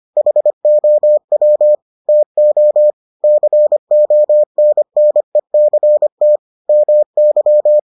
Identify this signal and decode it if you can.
Morse code, heard as a keyed tone, about 25 wpm, 600 Hz. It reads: HOW TO CONNECT MY